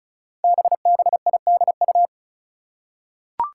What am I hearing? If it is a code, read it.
Morse code, heard as a keyed tone, about 36 wpm, 700 Hz. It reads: BBIDU
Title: Five Challenging Letter-Number Combinations 35wpm